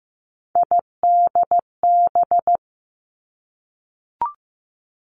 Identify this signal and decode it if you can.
Morse code, heard as a keyed tone, about 15 wpm, 700 Hz. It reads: IDB
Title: Three Challenging Letter-Number Combinations 15wpm